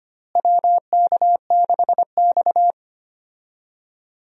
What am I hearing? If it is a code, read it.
Morse code, heard as a keyed tone, about 25 wpm, 700 Hz. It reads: WK6X
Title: USA Call Signs - 4 Characters - Repeated 2x - 25wpm